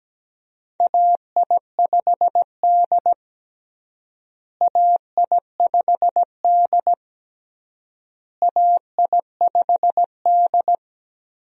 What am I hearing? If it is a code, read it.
Morse code, heard as a keyed tone, about 17 wpm, 700 Hz. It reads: AI5D AI5D AI5D
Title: USA Call Signs - 4 Characters - Repeated 3x - 17wpm